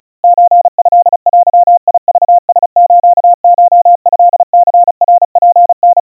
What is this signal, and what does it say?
Morse code (Japanese Wabun code), heard as a keyed tone, about 35 wpm, 700 Hz. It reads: ソトデクラスコトニナツタ
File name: kawa / Morse sample / kamyu 130 cpm 700 Hz Crispy